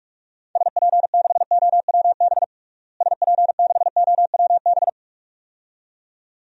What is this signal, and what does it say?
Morse code, heard as a keyed tone, about 46 wpm, 700 Hz. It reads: SP6OWB SP6OWB
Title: DXCC-100 Call Signs - 6 Characters - Repeated 2x - 45wpm